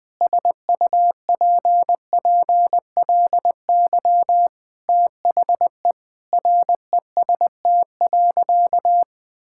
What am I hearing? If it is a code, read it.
Morse code, heard as a keyed tone, about 20 wpm, 700 Hz. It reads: SUPPLY THE REST.
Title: Sentences from Top 1000 Words - Review Part 5 - 20wpm